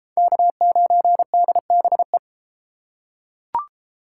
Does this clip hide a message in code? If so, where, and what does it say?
Yes: Morse code from 0.2 to 2.2 s, heard as a keyed tone, about 31 wpm, 700 Hz. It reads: K9DBE